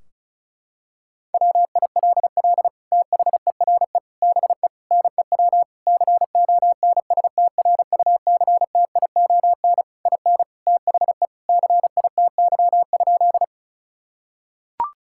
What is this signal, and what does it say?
Morse code, heard as a keyed tone, about 36 wpm, 700 Hz. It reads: WILL THERE BE NEW CONSTRUCTION IN THE CITY?
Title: Sentences from Top 1600 Words 35wpm